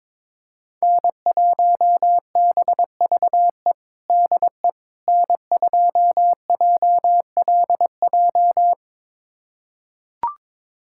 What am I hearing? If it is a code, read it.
Morse code, heard as a keyed tone, about 22 wpm, 700 Hz. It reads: N1BVE DE N2JLJ
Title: USA Callsign DE Callsign - 5 Characters - 22wpm